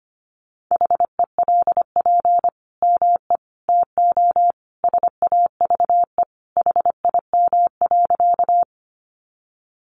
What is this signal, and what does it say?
Morse code, heard as a keyed tone, about 25 wpm, 700 Hz. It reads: HELP ME TO SAVE HIM.
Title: Sentences from Top 1100 Words 25wpm